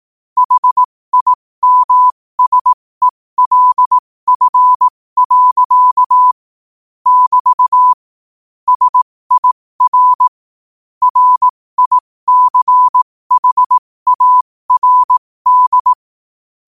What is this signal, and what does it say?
Morse code, heard as a keyed tone, about 18 wpm, spaced as for 15 wpm, 1000 Hz. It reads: HIMSELF. <BT> SIR RICHARD
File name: qotdfiles / 15wpm